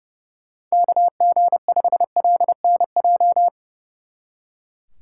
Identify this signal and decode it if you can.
Morse code, heard as a keyed tone, about 35 wpm, 700 Hz. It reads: KG5LNJ
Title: Speed Racing - USA Call Signs 30wpm